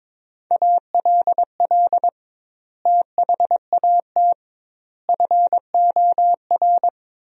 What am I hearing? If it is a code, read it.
Morse code, heard as a keyed tone, about 22 wpm, 700 Hz. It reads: ALL THAT FOR
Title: 2x Word-Spacing - Sentences from Top 500 Words - 22wpm